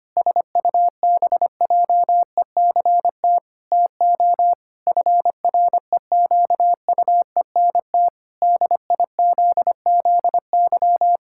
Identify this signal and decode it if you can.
Morse code, heard as a keyed tone, about 25 wpm, 700 Hz. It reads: SUBJECT TO FREQUENT DIZZY
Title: Pangram Challenge v2 - 25wpm